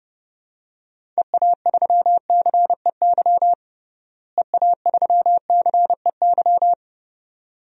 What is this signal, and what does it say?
Morse code, heard as a keyed tone, about 32 wpm, 700 Hz. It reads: EA3CEY EA3CEY
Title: DXCC-100 Call Signs - 6 Characters - Repeated 2x - 30wpm